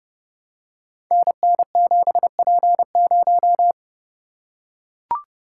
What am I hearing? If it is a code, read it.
Morse code, heard as a keyed tone, about 28 wpm, 700 Hz. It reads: NN7P0
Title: Code Groups - 5 Letter:Numbers - 30wpm